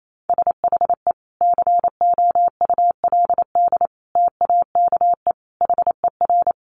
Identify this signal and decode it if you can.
Morse code, heard as a keyed tone, about 28 wpm, 690 Hz. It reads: SHE COULD TAKE HER
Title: Sentences from Top 900 Words 28wpm